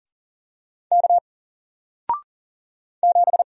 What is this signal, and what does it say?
Morse code, heard as a keyed tone, about 39 wpm, 700 Hz. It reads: K 7
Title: Single Character (Letter:Number:Punctuation:Prosign) Alt 40wpm